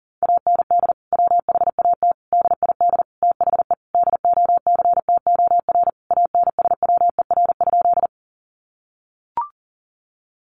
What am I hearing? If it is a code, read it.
Morse code, heard as a keyed tone, about 40 wpm, 700 Hz. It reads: AND WHAT DID THE DOCTOR ANSWER?
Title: Sentences from Top 1500 Words 40wpm